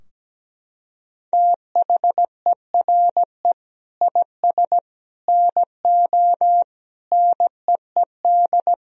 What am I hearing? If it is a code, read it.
Morse code, heard as a keyed tone, about 17 wpm, 700 Hz. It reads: THERE IS NO NEED
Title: Sentences from Top 1000 Words - Review Part 3 - 17wpm